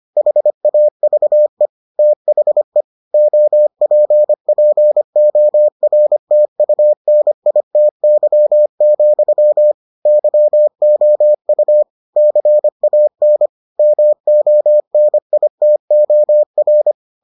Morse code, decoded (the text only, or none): HAVE THE OPPORTUNITY, YOU CAN MONITOR